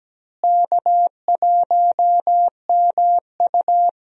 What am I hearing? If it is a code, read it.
Morse code, heard as a keyed tone, about 17 wpm, 700 Hz. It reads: K1MU
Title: USA Call Signs - 4 Characters - Repeated 3x - 17wpm